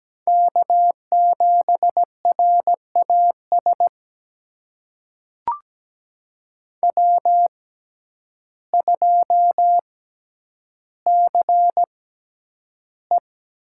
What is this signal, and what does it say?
Morse code, heard as a keyed tone, about 17 wpm, spaced as for 9 wpm, 700 Hz. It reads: K7RAS W2CE
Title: USA Call Signs - 6x Character Spacing - 17wpm